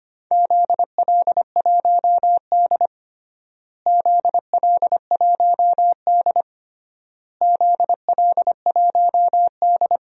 Morse code, heard as a keyed tone, about 25 wpm, 700 Hz. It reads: ZL1D ZL1D ZL1D
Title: DXCC-100 Call Signs - 4 Characters - Repeated 3x - 25wpm